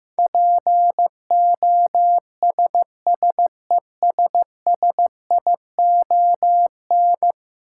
Morse code, heard as a keyed tone, about 16 wpm, 700 Hz. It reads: POSSESSION
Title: Sentences from Top 1400 Words - New Word Drill Alt - 15wpm